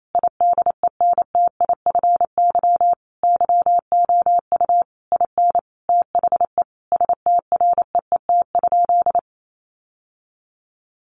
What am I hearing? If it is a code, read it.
Morse code, heard as a keyed tone, about 28 wpm, 700 Hz. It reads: IDENTIFY YOU IN THE STREET?
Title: Sentences from Top 900 Words 28wpm